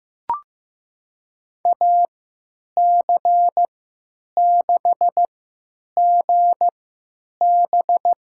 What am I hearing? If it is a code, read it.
Morse code, heard as a keyed tone, about 16 wpm, spaced as for 9 wpm, 700 Hz. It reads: AC6GB
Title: USA Call Signs - 5 Characters - 3x Character Spacing - 15wpm